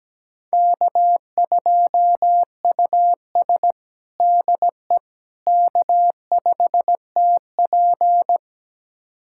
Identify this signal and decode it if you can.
Morse code, heard as a keyed tone, about 17 wpm, 700 Hz. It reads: K2US DE K5TP
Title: USA Callsign DE Callsign - 4 Characters - 17wpm